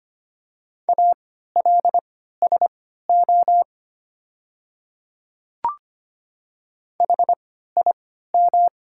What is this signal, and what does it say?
Morse code, heard as a keyed tone, about 25 wpm, spaced as for 14 wpm, 700 Hz. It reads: ALSO HIM
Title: 3x Character Spacing - Top 200 Words - 25wpm